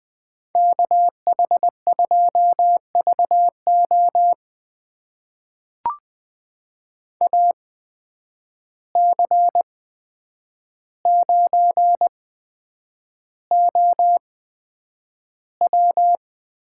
Morse code, heard as a keyed tone, about 20 wpm, spaced as for 9 wpm, 700 Hz. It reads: KH2VO AC9OW
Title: USA Call Signs - 5 Characters - 8x Character Spacing - 20wpm